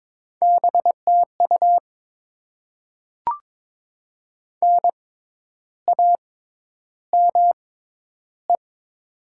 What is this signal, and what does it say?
Morse code, heard as a keyed tone, about 22 wpm, spaced as for 10 wpm, 700 Hz. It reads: BTU NAME
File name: Top 20 QSO Elements - 6x Character Spacing - 22wpm